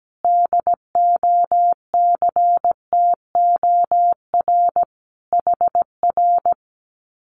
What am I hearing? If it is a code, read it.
Morse code, heard as a keyed tone, about 17 wpm, 700 Hz. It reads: DOCTOR HR
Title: Common QSO Phrases Encore 17wpm